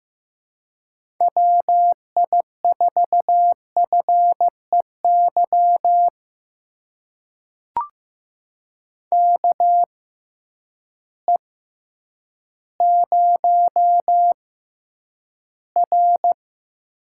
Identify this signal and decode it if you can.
Morse code, heard as a keyed tone, about 15 wpm, spaced as for 9 wpm, 700 Hz. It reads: WI4FEY KE0R
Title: USA Call Signs - 6 Characters - 6x Character Spacing - 15wpm